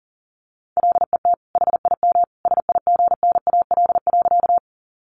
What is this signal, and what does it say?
Morse code, heard as a keyed tone, about 40 wpm, 700 Hz. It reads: LET HIM SIGNAL.
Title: Sentences from Top 1400 Words 40wpm